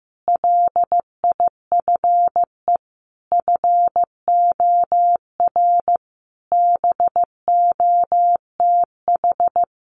Morse code, heard as a keyed tone, about 15 wpm, 700 Hz. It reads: LIFE FOR BOTH
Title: Sentences from Top 1100 Words 15wpm